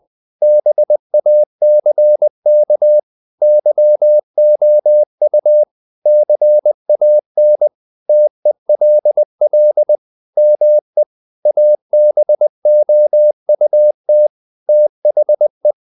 Morse code, heard as a keyed tone, about 20 wpm, 600 Hz. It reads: BACK YOU CAN TELL ME ABOUT THE
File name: the old man and the sea / the old man and the sea part 1 20wpm